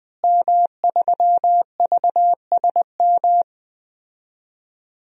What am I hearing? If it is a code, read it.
Morse code, heard as a keyed tone, about 21 wpm, 700 Hz. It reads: M3VSM
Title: DXCC-100 Call Signs - 5 Characters - Repeated 3x - 20wpm